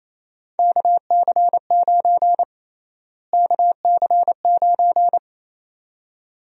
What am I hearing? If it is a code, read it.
Morse code, heard as a keyed tone, about 28 wpm, 700 Hz. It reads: KC9 KC9
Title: USA Call Sign Prefixes - Repeated 2x - 28wpm